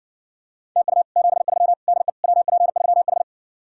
Morse code, heard as a keyed tone, about 60 wpm, 700 Hz. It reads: TU 73 DE WP3H